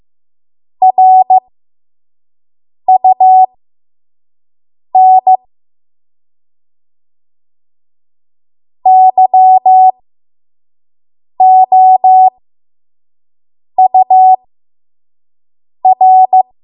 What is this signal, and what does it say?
Morse code, heard as a keyed tone, about 15 wpm, spaced as for 5 wpm, 750 Hz. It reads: RUN YOUR
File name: Morse / 05WPM